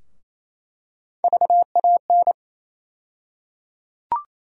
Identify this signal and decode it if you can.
Morse code, heard as a keyed tone, about 28 wpm, 700 Hz. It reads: VAN
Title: USA Call Sign Postfixes - 3 Letters 28wpm